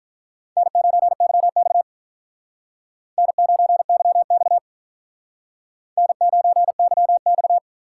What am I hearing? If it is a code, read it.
Morse code, heard as a keyed tone, about 44 wpm, 700 Hz. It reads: N9YX N9YX N9YX